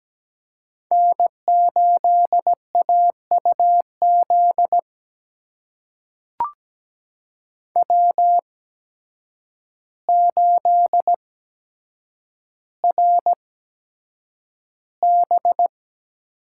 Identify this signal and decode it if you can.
Morse code, heard as a keyed tone, about 17 wpm, spaced as for 8 wpm, 700 Hz. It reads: N8AUZ W8RB
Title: USA Call Signs - 8x Character Spacing - 17wpm